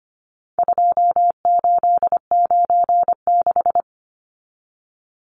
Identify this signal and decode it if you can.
Morse code, heard as a keyed tone, about 25 wpm, 700 Hz. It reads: 2896
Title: Numbers - 4-Digit 25wpm